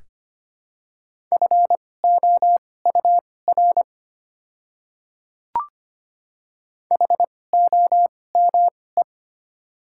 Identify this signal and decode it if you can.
Morse code, heard as a keyed tone, about 25 wpm, spaced as for 18 wpm, 700 Hz. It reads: FOUR HOME
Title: 2x Character Spacing - Top 300 Words - 25wpm